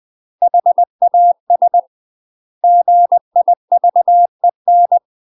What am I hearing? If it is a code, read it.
Morse code, heard as a keyed tone, about 20 wpm, 700 Hz. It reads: HAS GIVEN
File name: UK MASTER QSOs TRAINING / 20WPM / UK MASTER QSOs 0013